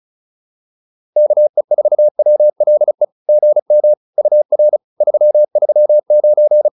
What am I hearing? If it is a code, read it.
Morse code, heard as a keyed tone, about 34 wpm, 600 Hz. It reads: KE4WLE GM UR 339